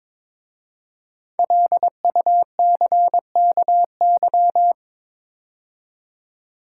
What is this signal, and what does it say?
Morse code, heard as a keyed tone, about 23 wpm, 700 Hz. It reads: LUCKY
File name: 5 Letter Words - First 3 Letters Given - 22wpm